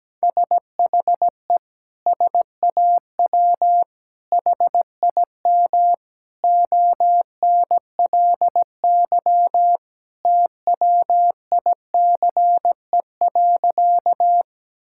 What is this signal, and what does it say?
Morse code, heard as a keyed tone, about 17 wpm, 700 Hz. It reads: SHE SAW HIM ONLY TWICE.
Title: Sentences from Top 1600 Words 17wpm